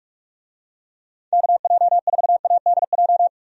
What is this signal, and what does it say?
Morse code, heard as a keyed tone, about 46 wpm, 700 Hz. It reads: KJ4ADJ
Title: USA Call Signs - 6 Characters - 45wpm